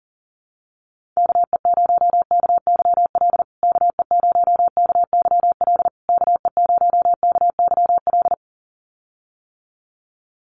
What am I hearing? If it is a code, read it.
Morse code, heard as a keyed tone, about 40 wpm, 700 Hz. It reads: KE0KYL KE0KYL KE0KYL
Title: USA Call Signs Repeated 3x 40wpm